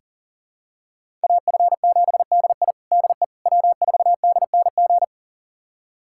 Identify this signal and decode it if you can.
Morse code, heard as a keyed tone, about 40 wpm, 700 Hz. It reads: AF7DI DE W4DNG
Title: USA Callsign DE Callsign - 5 Characters - 40wpm